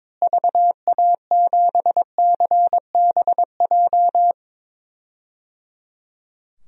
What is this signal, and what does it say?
Morse code, heard as a keyed tone, about 22 wpm, 700 Hz. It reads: VA7CBJ
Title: DXCC-100 Call Signs - 6 Characters - Repeated 3x - 22wpm